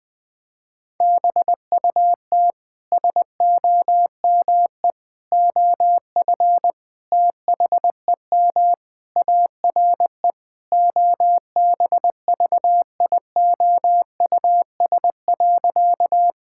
Morse code, heard as a keyed tone, about 21 wpm, 700 Hz. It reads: BUT SOME OF THEM ARE OBVIOUS.
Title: Sentences from Top 1600 Words 20wpm